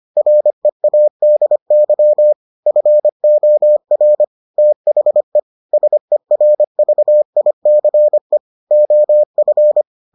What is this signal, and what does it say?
Morse code, heard as a keyed tone, about 25 wpm, 600 Hz. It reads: READY FOR THE SERVICE OF